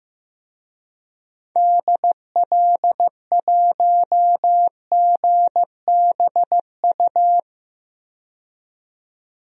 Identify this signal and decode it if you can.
Morse code, heard as a keyed tone, about 15 wpm, 700 Hz. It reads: DL1GBU
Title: DXCC-100 Call Signs - 6 Characters - 15wpm